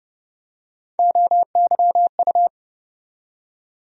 Morse code, heard as a keyed tone, about 30 wpm, 700 Hz. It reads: OYU
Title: USA Call Sign Postfixes - 3 Letters 30wpm